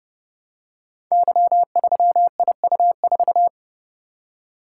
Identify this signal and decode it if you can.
Morse code, heard as a keyed tone, about 30 wpm, 700 Hz. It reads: Y3IU4
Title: Code Groups - 5 Letter:Numbers - 30wpm